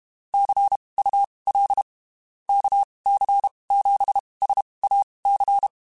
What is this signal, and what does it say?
Morse code, heard as a keyed tone, about 31 wpm, spaced as for 23 wpm, 800 Hz. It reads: CUL KC7SAC
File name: (spaced) / QSOs Spaced-mp3s / kc7sac @30 sp